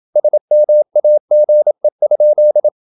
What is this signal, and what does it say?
Morse code, heard as a keyed tone, about 27 wpm, 600 Hz. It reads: SMAGE?
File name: Ferie QRQ 27wpm